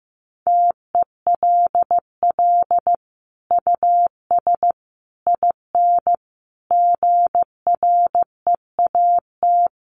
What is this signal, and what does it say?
Morse code, heard as a keyed tone, about 15 wpm, 700 Hz. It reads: TELL US IN GREAT
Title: Sentences from Top 1500 Words 15wpm